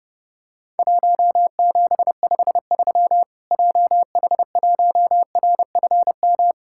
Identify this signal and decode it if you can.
Morse code, heard as a keyed tone, about 30 wpm, 700 Hz. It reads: 1753 JH1RFM
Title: Worked All Europe DX Contest - QTC Exchange - 30wpm